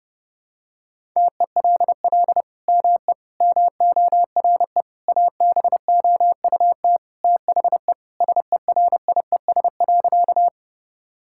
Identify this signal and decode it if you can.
Morse code, heard as a keyed tone, about 30 wpm, 700 Hz. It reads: TELL ME MORE ABOUT THE SERIES.
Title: Sentences from Top 1000 Words - Review Part 5 - 30wpm